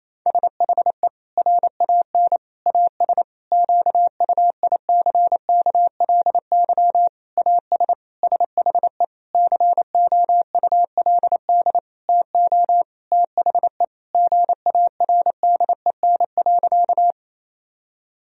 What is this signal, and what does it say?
Morse code, heard as a keyed tone, about 28 wpm, 700 Hz. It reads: SHE RAN AS QUICKLY AS SHE COULD TO THE GARDEN.
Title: Sentences from Top 1000 Words - Review Part 4 - 28wpm